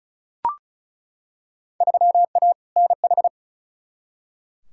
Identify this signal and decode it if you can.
Morse code, heard as a keyed tone, about 36 wpm, 700 Hz. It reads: 3A NH
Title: Field Day Exchanges Encore - 35wpm